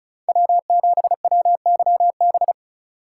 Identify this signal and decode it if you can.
Morse code, heard as a keyed tone, about 34 wpm, 700 Hz. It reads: W7WYB
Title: USA Call Signs - 5 Characters - 3x Character Spacing - 35wpm